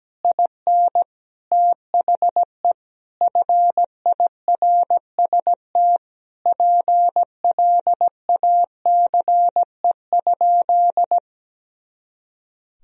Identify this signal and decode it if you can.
Morse code, heard as a keyed tone, about 17 wpm, 700 Hz. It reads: IN THE FIRST PLACE?
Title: Sentences from Top 200 Words and Abbreviations 17wpm